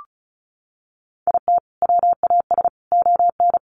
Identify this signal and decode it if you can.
Morse code, heard as a keyed tone, about 34 wpm, 700 Hz. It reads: IT WAS ON
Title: Sentences from Top 800 Words 35wpm